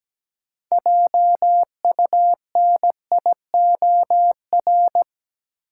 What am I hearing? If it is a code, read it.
Morse code, heard as a keyed tone, about 17 wpm, 700 Hz. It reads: JUNIOR
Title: Sentences from Top 1500 Words - New Word Drill Alt - 17wpm